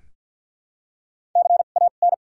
Morse code, heard as a keyed tone, about 53 wpm, spaced as for 37 wpm, 700 Hz. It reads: CAN
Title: 2x Character Spacing - Top 300 Words - 50wpm